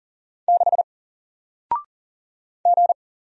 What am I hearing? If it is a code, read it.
Morse code, heard as a keyed tone, about 40 wpm, 700 Hz. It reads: 6 G
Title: Single Letter-Number Alt 40wpm